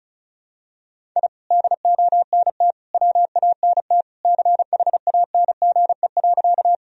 Morse code, heard as a keyed tone, about 36 wpm, 700 Hz. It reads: I DONT WANT CHANGE.
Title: Sentences from Top 1000 Words - Review Part 1 - 35wpm